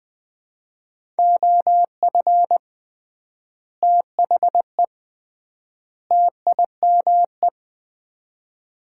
Morse code, heard as a keyed tone, about 20 wpm, 700 Hz. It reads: OF THE TIME
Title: 3x Word-Spacing - Sets of 3 Words - 20wpm